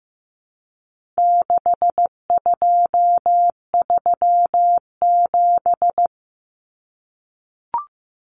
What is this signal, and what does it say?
Morse code, heard as a keyed tone, about 15 wpm, 700 Hz. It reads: 6237
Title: Numbers - 4-Digit 15wpm